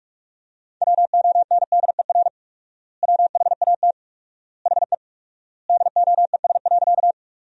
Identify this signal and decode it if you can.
Morse code, heard as a keyed tone, about 45 wpm, 700 Hz. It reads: WONDER WHAT HE DOES.
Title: 4x Word-Spacing - Sentences from Top 500 Words - 45wpm